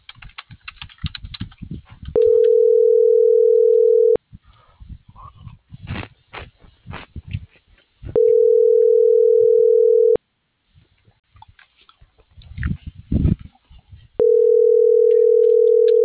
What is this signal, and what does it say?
Ringback tone (North America), 440 + 480 Hz.